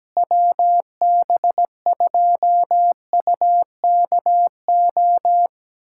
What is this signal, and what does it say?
Morse code, heard as a keyed tone, about 17 wpm, 700 Hz. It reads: WB2UKO
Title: USA Callsign DE Callsign - 6 Characters - 17wpm